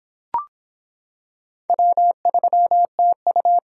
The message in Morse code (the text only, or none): W3TU